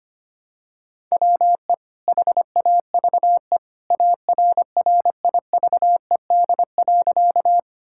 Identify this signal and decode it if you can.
Morse code, heard as a keyed tone, about 25 wpm, 700 Hz. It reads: WE HAVE ARRIVED.
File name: practice / 25wpm / Sentences with 2-Word Phrases - 25wpm